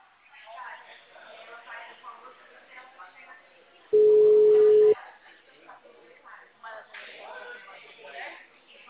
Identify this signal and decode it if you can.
Ringback tone (Europe, 425 Hz), with speech.